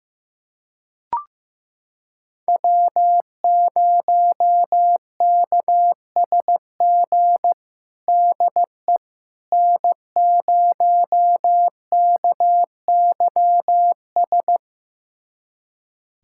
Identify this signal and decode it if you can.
Morse code, heard as a keyed tone, about 16 wpm, 700 Hz. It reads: W0KSG DE N0KYS
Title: USA Callsign DE Callsign - 5 Characters - 15wpm